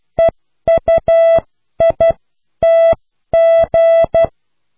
Morse code, heard as a keyed tone, about 12 wpm, spaced as for 10 wpm, 660 Hz. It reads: EUITG